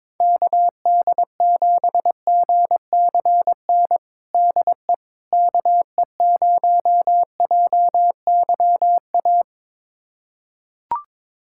Morse code, heard as a keyed tone, about 21 wpm, 700 Hz. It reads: KD7GCN DE KE0JYA
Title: USA Callsign DE Callsign 22wpm